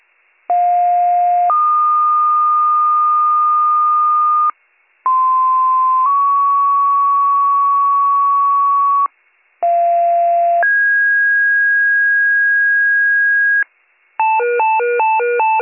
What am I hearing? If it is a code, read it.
Two-tone page, heard as a one-second tone, then a three-second tone, about 710 then 1220 Hz, about 1040 then 1120 Hz, about 690 then 1650 Hz. It is heard from 0.5 to 13.6 s.